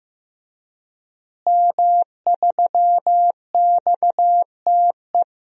Morse code, heard as a keyed tone, about 15 wpm, 700 Hz. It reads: M3XTE
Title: DXCC-100 Call Signs - 5 Characters - Repeated 3x - 15wpm